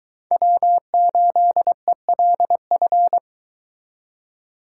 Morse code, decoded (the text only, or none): W8ELF